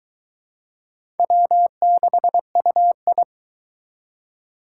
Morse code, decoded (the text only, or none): W6UI